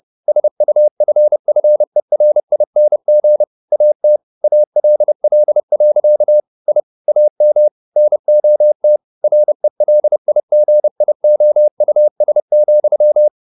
Morse code, heard as a keyed tone, about 30 wpm, 600 Hz. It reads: SUFFERING AT ALL. I AM NOT RELIGIOUS,
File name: the old man and the sea / the old man and the sea 1 30wpm